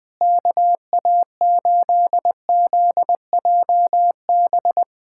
Morse code, heard as a keyed tone, about 20 wpm, 700 Hz. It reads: KA8ZJB
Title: USA Callsign DE Callsign - 6 Characters - 20wpm